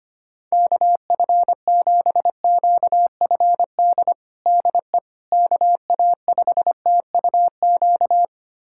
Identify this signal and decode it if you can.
Morse code, heard as a keyed tone, about 25 wpm, 700 Hz. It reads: KF7QFD DE KA5TUQ
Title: USA Callsign DE Callsign - 6 Characters - 25wpm